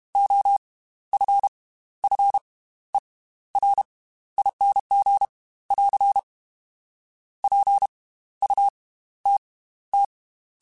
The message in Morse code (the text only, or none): OFFERING <AR> PUTT